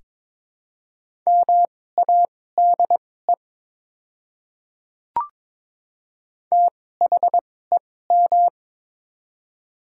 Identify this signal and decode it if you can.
Morse code, heard as a keyed tone, about 21 wpm, spaced as for 16 wpm, 700 Hz. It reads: MADE THEM
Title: 2x Character Spacing - Top 200 Words - 22wpm